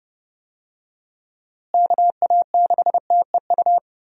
Morse code, heard as a keyed tone, about 32 wpm, 700 Hz. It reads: KA6TEU
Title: USA Call Signs - 6 Characters - Repeated 2x - 30wpm